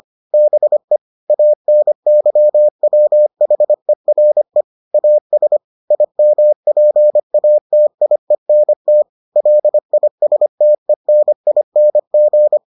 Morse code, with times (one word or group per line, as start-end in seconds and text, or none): 0.3-1.0 BE
1.3-4.6 ANYWHERE
4.9-5.6 AS
5.9-9.0 IMPATIENT
9.4-12.6 LISTENING